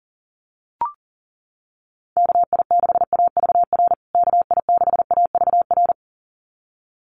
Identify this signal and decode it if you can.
Morse code, heard as a keyed tone, about 37 wpm, 700 Hz. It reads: KI6AVR KI6AVR
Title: USA Call Signs Repeated 2x 40wpm